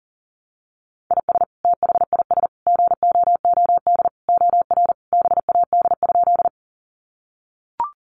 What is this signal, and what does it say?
Morse code, heard as a keyed tone, about 40 wpm, 700 Hz. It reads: IS THIS GOOD OR BAD?
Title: Sentences from Top 700 Words 40wpm